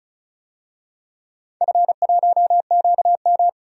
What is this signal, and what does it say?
Morse code, heard as a keyed tone, about 34 wpm, 700 Hz. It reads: F1QM
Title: DXCC-100 Call Signs - 4 Characters - 35wpm